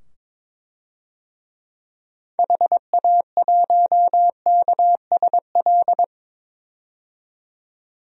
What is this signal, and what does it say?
Morse code, heard as a keyed tone, about 22 wpm, 700 Hz. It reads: HA1KSL
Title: DXCC-100 Call Signs - 6 Characters - 22wpm